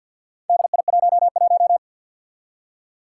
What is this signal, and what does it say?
Morse code, heard as a keyed tone, about 51 wpm, 700 Hz. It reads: DI11
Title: Four Challenging Letter-Number Combinations 50wpm